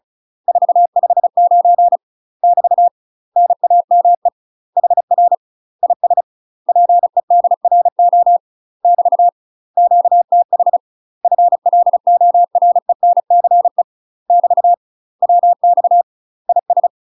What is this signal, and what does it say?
Morse code, heard as a keyed tone, about 34 wpm, 700 Hz. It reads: 459 <BT> NAME HR IS PEDRO <BT> QTH FLORENCE <BT> WX IS
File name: UK MASTER QSOs TRAINING / 35WPM / UK MASTER QSOs 0025